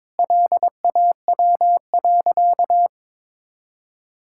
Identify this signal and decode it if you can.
Morse code, heard as a keyed tone, about 21 wpm, 700 Hz. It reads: LAW.
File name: Sentences with 7-Word Phrases - 22wpm